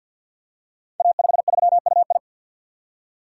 Morse code, heard as a keyed tone, about 49 wpm, 700 Hz. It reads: A53UI
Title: Code Groups - 5 Letter:Numbers - 50wpm